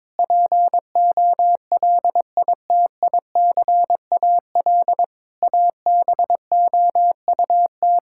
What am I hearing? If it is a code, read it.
Morse code, heard as a keyed tone, about 22 wpm, 700 Hz. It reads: POLITICAL ABOUT
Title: Sentences from Top 1000 Words - Review Part 5 - 22wpm